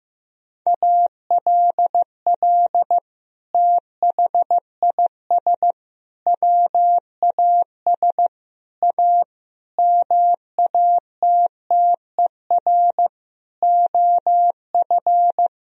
Morse code, heard as a keyed tone, about 15 wpm, 700 Hz. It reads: ALL THIS WAS A MATTER OF
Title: Sentences from Top 1000 Words - Review Part 4 - 15wpm